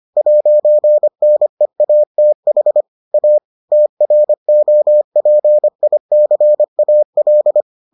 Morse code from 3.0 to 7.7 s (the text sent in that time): A TROPICAL